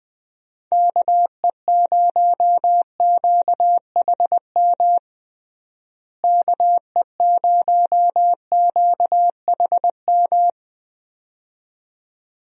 Morse code, heard as a keyed tone, about 20 wpm, 700 Hz. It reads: KE0QHM KE0QHM
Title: USA Call Signs - 6 Characters - Repeated 2x - 20wpm